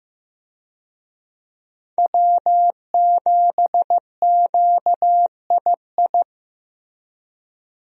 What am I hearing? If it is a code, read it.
Morse code, heard as a keyed tone, about 15 wpm, 700 Hz. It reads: W7QII
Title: USA Call Signs - 5 Characters - Repeated 2x - 15wpm